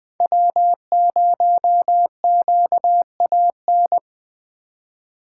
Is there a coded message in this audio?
Yes: Morse code from 0.2 to 4.0 s, heard as a keyed tone, about 21 wpm, 700 Hz. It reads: W0QAN